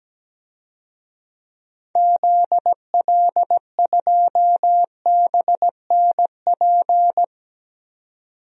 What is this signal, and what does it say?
Morse code, heard as a keyed tone, about 17 wpm, 700 Hz. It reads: ZL2BNP
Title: DXCC-100 Call Signs - 6 Characters - 17wpm